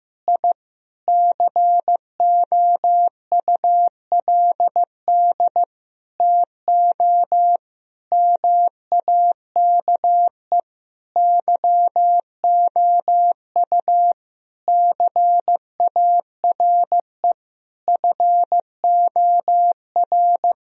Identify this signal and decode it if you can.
Morse code, heard as a keyed tone, about 15 wpm, 700 Hz. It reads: I COULD TO MAKE YOU CARE FOR
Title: Sentences from Top 1000 Words - Review Part 1 - 15wpm